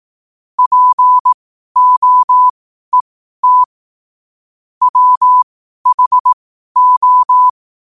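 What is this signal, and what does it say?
Morse code, heard as a keyed tone, about 18 wpm, spaced as for 12 wpm, 1000 Hz. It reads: POET WHO
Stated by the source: Morse code sent at 12 words per minute.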